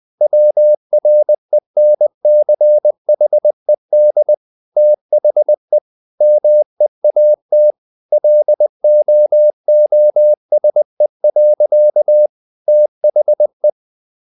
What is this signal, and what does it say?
Morse code, heard as a keyed tone, about 20 wpm, 600 Hz. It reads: WRENCHED THE MEAT LOOSE. THE
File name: the old man and the sea part 2 20wpm